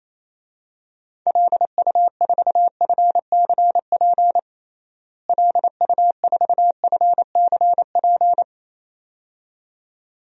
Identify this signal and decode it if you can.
Morse code, heard as a keyed tone, about 28 wpm, 700 Hz. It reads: LU4FCP LU4FCP
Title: DXCC-100 Call Signs - 6 Characters - Repeated 2x - 28wpm